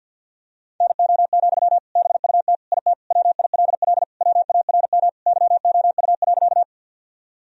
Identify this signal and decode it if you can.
Morse code, heard as a keyed tone, about 52 wpm, 700 Hz. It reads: NO, BUT IT WILL WARM YOU.
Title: Sentences from Top 1000 Words - Review Part 2 - 50wpm